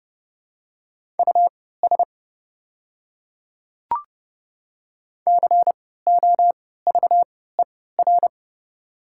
Morse code, heard as a keyed tone, about 32 wpm, spaced as for 17 wpm, 700 Hz. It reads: US COVER
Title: 3x Character Spacing - Top 300 Words - 30wpm